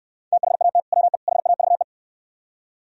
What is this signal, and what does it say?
Morse code, heard as a keyed tone, about 67 wpm, 700 Hz. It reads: THAT WE HAVE